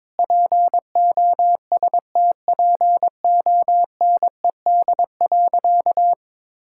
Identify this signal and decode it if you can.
Morse code, heard as a keyed tone, about 22 wpm, 700 Hz. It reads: POSTPONED.
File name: practice / 22wpm / Sentences with 8-Word Phrases - 22wpm